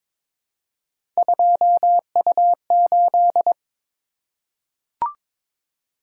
Morse code, heard as a keyed tone, about 22 wpm, 700 Hz. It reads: 2U8
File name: practice / 22wpm / Three Letter-Number Combinations - 22wpm